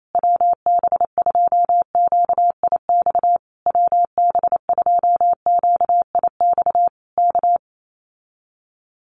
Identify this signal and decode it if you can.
Morse code, heard as a keyed tone, about 28 wpm, 700 Hz. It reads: WB2QIX WB2QIX K
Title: Common QSO Phrases Encore 28wpm